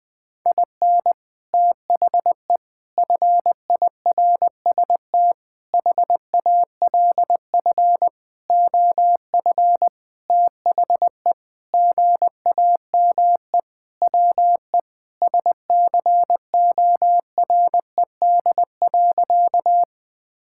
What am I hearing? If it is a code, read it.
Morse code, heard as a keyed tone, about 20 wpm, 700 Hz. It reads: IN THE FIRST HALF OF THE GAME WE SCORED.
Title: Sentences with 6-Word Phrases - 20wpm